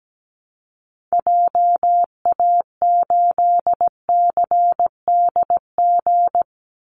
Morse code, heard as a keyed tone, about 17 wpm, 700 Hz. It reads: JA8CDG
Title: Call Signs Repeated 3x - 17wpm